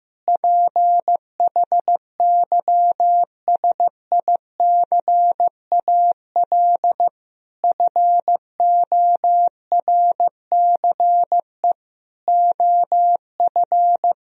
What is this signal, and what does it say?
Morse code, heard as a keyed tone, about 16 wpm, 700 Hz. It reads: PHYSICAL FORCE OF
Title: Sentences from Top 1000 Words - Review Part 4 - 15wpm